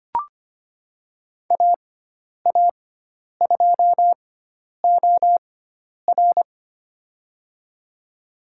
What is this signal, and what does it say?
Morse code, heard as a keyed tone, about 25 wpm, spaced as for 10 wpm, 700 Hz. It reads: AA2OR
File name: Lesson 16g - Letter c - Call Signs 25wpm